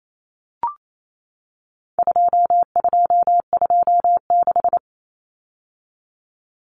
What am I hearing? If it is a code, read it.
Morse code, heard as a keyed tone, about 28 wpm, 700 Hz. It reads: 2226
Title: Common Cut Numbers - 4-Digit 28wpm